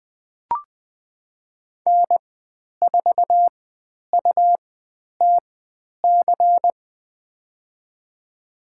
Morse code, heard as a keyed tone, about 20 wpm, spaced as for 10 wpm, 700 Hz. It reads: N4UTC
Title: 109 Lesson 18g - Letter m - Call Signs 20wpm